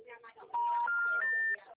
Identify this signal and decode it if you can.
Special information tone, with speech.